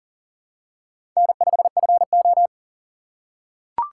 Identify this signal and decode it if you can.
Morse code, heard as a keyed tone, about 44 wpm, 690 Hz. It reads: N5FO
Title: USA Call Signs - 4 Characters - 2x Character Spacing - 40wpm